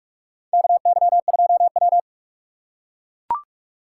Morse code, heard as a keyed tone, about 41 wpm, 700 Hz. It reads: KY2W
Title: USA Call Signs - 3x Character Spacing - 45wpm